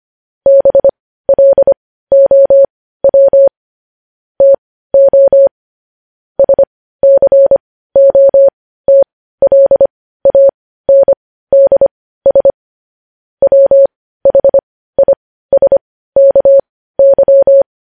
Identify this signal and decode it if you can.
Morse code, heard as a keyed tone, about 25 wpm, spaced as for 15 wpm, 550 Hz. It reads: BLOW TO SCOTLANDS WHISKY
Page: news headlines in morse code at 15 wpm